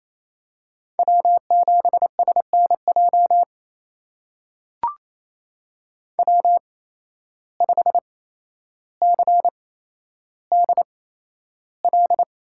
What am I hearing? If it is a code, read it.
Morse code, heard as a keyed tone, about 28 wpm, spaced as for 12 wpm, 700 Hz. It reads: W7SNJ W5CDL
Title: USA Call Signs - 5 Characters - 8x Character Spacing - 28wpm